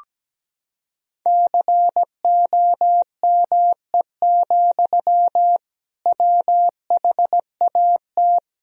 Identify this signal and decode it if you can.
Morse code, heard as a keyed tone, about 17 wpm, 700 Hz. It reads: COME, WHAT
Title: Sentences from Top 1000 Words - Review Part 3 - 17wpm